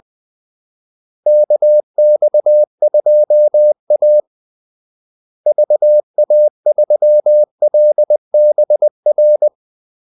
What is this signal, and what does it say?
Morse code, heard as a keyed tone, about 20 wpm, 600 Hz. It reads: KX2A VA3LBR
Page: Kaldesignaler 20 wpm